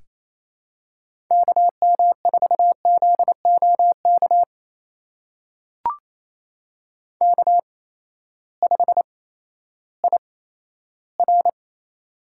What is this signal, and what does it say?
Morse code, heard as a keyed tone, about 28 wpm, spaced as for 14 wpm, 700 Hz. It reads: KM4ZOK K5IR
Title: USA Call Signs - 8x Character Spacing - 28wpm